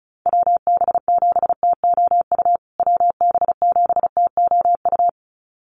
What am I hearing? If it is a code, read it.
Morse code, heard as a keyed tone, about 34 wpm, 700 Hz. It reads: WB7TOU WB7TOU
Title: USA Call Signs Repeated 2x 35wpm